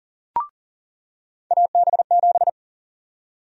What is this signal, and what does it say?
Morse code, heard as a keyed tone, about 40 wpm, 700 Hz. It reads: AB7
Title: USA Call Sign Prefixes - 40wpm